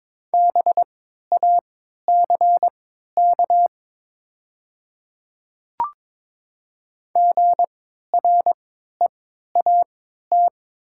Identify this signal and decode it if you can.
Morse code, heard as a keyed tone, about 22 wpm, spaced as for 12 wpm, 700 Hz. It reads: BACK GREAT
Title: 3x Character Spacing - Top 200 Words - 22wpm